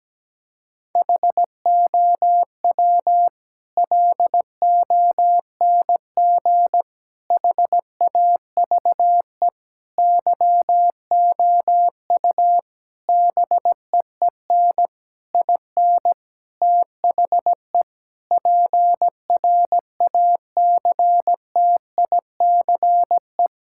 Morse code, heard as a keyed tone, about 17 wpm, 700 Hz. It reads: HOW LONG HAVE YOU BEEN IN THE PRACTICE
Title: Sentences from Top 1000 Words - Review Part 6 - 17wpm